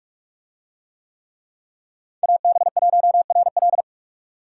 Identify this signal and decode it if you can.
Morse code, heard as a keyed tone, about 41 wpm, 700 Hz. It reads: AB1RL